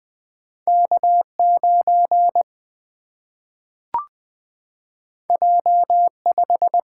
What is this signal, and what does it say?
Morse code, heard as a keyed tone, about 20 wpm, 700 Hz. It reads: K9 J5
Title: Single-Letter Single-Number 20wpm